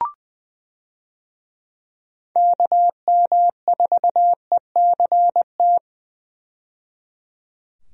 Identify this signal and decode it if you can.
Morse code, heard as a keyed tone, about 20 wpm, 700 Hz. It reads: KM4ECT
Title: USA Call Signs - 6 Characters - 20wpm